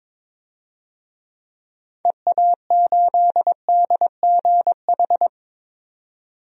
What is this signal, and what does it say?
Morse code, heard as a keyed tone, about 22 wpm, 700 Hz. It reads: EA8DGH
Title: DXCC-100 Call Signs - 6 Characters - 22wpm